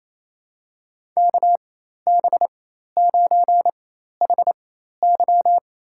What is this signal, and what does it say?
Morse code, heard as a keyed tone, about 28 wpm, spaced as for 13 wpm, 700 Hz. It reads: KB9HY
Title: USA Call Signs - 4x Character Spacing - 28wpm